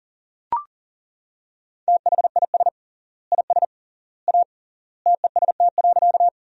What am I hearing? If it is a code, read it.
Morse code, heard as a keyed tone, about 40 wpm, 700 Hz. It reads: THIS IS A TEST.
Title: 3x Word-Spacing - Sentences from Top 500 Words - 40wpm